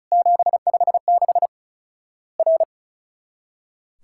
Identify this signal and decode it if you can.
Morse code, heard as a keyed tone, about 36 wpm, 700 Hz. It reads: 756 R
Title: Worked All Europe DX Contest - QTC Exchange - 35wpm